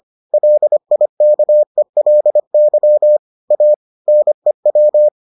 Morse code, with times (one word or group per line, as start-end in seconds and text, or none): 0.3-3.2 LIKELY
3.5-3.7 A
4.1-5.1 NEW